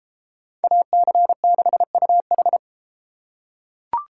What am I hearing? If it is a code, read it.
Morse code, heard as a keyed tone, about 32 wpm, 700 Hz. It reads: AC6UH